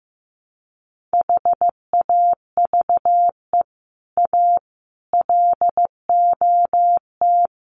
Morse code, heard as a keyed tone, about 15 wpm, 700 Hz. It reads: HAVE A LOT
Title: Sets of 4 Words 15wpm